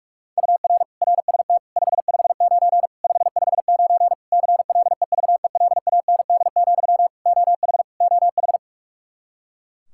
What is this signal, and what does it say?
Morse code, heard as a keyed tone, about 41 wpm, 700 Hz. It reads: UR RST 559 559 CLEVELAND, OH OH
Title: K1USN SST QSO Practice 45wpm